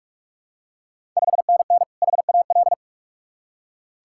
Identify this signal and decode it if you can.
Morse code, heard as a keyed tone, about 46 wpm, 690 Hz. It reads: 5NN HAL
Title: Ontario QSO Party Exchange 45wpm